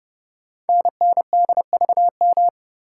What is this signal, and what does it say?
Morse code, heard as a keyed tone, about 30 wpm, 700 Hz. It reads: NNDVM
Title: Code Groups - 5 Letters - 30wpm